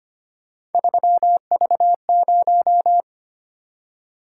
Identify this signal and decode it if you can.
Morse code, heard as a keyed tone, about 25 wpm, 700 Hz. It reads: 3V0
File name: Three Letter-Number Combinations - 25wpm